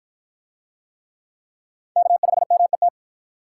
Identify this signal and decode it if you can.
Morse code, heard as a keyed tone, about 53 wpm, 700 Hz. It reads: K5MET